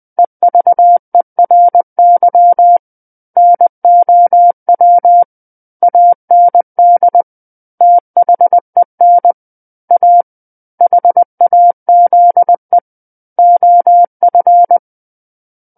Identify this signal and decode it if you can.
Morse code, heard as a keyed tone, about 20 wpm, 700 Hz. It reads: EVERY NOW AND THEN A HAZE OF
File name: 20 WPM